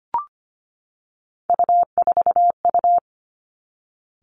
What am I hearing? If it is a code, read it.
Morse code, heard as a keyed tone, about 25 wpm, 700 Hz. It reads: U4U
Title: Three Challenging Letter-Number Combinations 25wpm